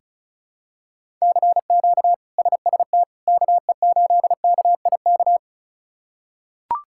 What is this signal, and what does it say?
Morse code, heard as a keyed tone, about 34 wpm, 700 Hz. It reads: CQ SST KE8KIK
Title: K1USN SST QSO Practice - 35wpm